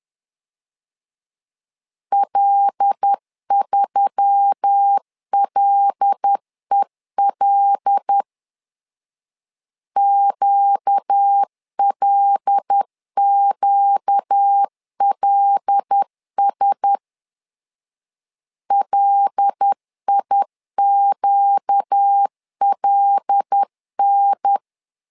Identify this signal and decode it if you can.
Morse code, heard as a keyed tone, about 10 wpm, 800 Hz. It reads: L3LEL QLQLS LIQLN
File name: LEZIONI / 5 LBVC / 5 L 10wpm 2259